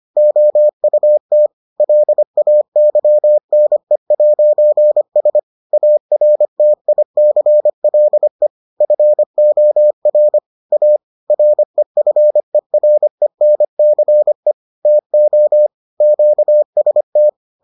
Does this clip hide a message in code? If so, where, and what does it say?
Yes: Morse code from 0.2 to 17.3 s, heard as a keyed tone, about 25 wpm, 600 Hz. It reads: OUT LAYNE'S ARTICLE FOR A REFERENCE TO QST